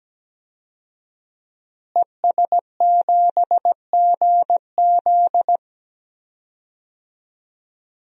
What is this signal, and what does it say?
Morse code, heard as a keyed tone, about 17 wpm, 700 Hz. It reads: ES7GZ
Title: DXCC-100 Call Signs - 5 Characters - Repeated 2x - 17wpm